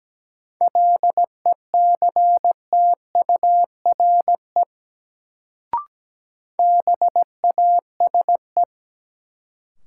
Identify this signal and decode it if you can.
Morse code, heard as a keyed tone, about 17 wpm, 700 Hz. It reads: LECTURE BASE
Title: Sentences from Top 1600 Words - New Word Drill Alt - 17wpm